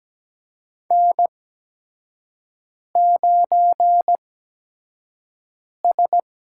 Morse code, heard as a keyed tone, about 17 wpm, spaced as for 5 wpm, 700 Hz. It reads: N9S
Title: USA Call Signs - 5 Characters - 8x Character Spacing - 17wpm